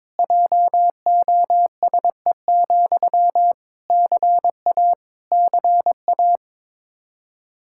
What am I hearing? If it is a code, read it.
Morse code, heard as a keyed tone, about 22 wpm, 700 Hz. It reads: JOSE, CA CA
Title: Practice QSO - Exchange 1 - Beginning - 22wpm